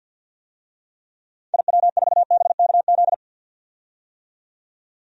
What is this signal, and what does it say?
Morse code, heard as a keyed tone, about 52 wpm, 700 Hz. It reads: IW4BKZ
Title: DXCC-100 Call Signs - 6 Characters - 50wpm